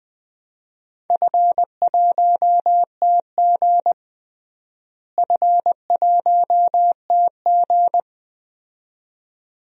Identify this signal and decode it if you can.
Morse code, heard as a keyed tone, about 20 wpm, 700 Hz. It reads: F1TG F1TG
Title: DXCC-100 Call Signs - 4 Characters - Repeated 2x - 20wpm